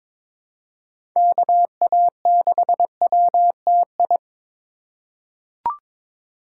Morse code, heard as a keyed tone, about 22 wpm, 700 Hz. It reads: KA6WTI
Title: USA Call Signs - 6 Characters - 2x Character Spacing - 22wpm